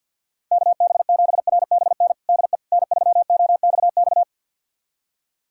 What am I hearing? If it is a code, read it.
Morse code, heard as a keyed tone, about 52 wpm, 700 Hz. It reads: KB7RBN DE N3OXX
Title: USA Callsign DE Callsign 50wpm